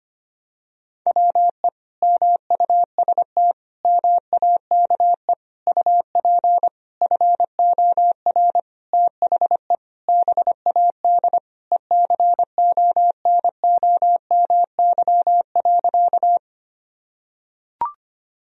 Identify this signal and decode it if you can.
Morse code, heard as a keyed tone, about 25 wpm, 700 Hz. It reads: WE MUST MAKE UP FOR THE BAD ECONOMY.
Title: Sentences from Top 1000 Words - Review Part 3 - 25wpm